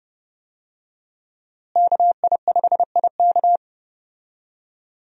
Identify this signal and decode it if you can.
Morse code, heard as a keyed tone, about 31 wpm, 700 Hz. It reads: KI5IK
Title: USA Call Signs - 5 Characters - Repeated 2x - 30wpm